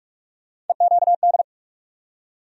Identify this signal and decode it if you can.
Morse code, heard as a keyed tone, about 43 wpm, 700 Hz. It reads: EQD
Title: USA Call Sign Postfixes - 3 Letters 45wpm